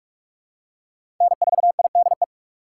Morse code, heard as a keyed tone, about 47 wpm, 700 Hz. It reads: N4IDE
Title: USA Call Signs - 5 Characters - 6x Character Spacing - 45wpm